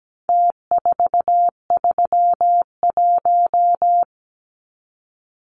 Morse code, heard as a keyed tone, about 17 wpm, 700 Hz. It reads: T431
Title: Numbers - Contest Style - 4-Digit 17wpm